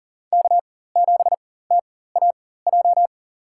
Morse code, heard as a keyed tone, about 40 wpm, spaced as for 19 wpm, 700 Hz. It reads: K7TAJ
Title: USA Call Signs - 5 Characters - 4x Character Spacing - 40wpm